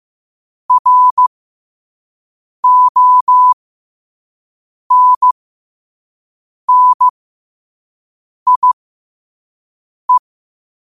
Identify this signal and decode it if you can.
Morse code, heard as a keyed tone, about 15 wpm, spaced as for 5 wpm, 1000 Hz. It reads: RONNIE
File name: qotdfiles / 5wpm